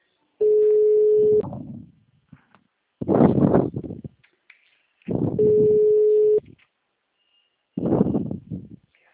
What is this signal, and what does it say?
Ringback tone (Europe, 425 Hz).